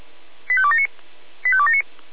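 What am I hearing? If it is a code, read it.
Five-tone selective call (ZVEI), heard as quick runs of five tones, tones of about 70 ms. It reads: 85279 85279
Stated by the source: Five-tone ZVEI selective call received on an SDR, fire brigade alerting.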